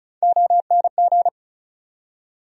Morse code, heard as a keyed tone, about 34 wpm, 700 Hz. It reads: ONG
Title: USA Call Sign Postfixes - 3 Letters 35wpm